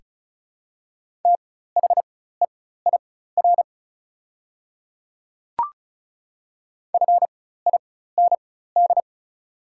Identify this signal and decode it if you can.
Morse code, heard as a keyed tone, about 34 wpm, spaced as for 16 wpm, 700 Hz. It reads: THEIR FIND
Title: 4x Character Spacing - Top 100 Words - 35wpm